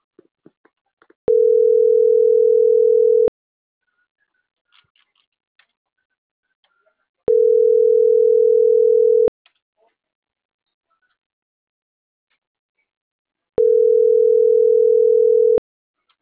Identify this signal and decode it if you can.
Ringback tone (North America), 440 + 480 Hz.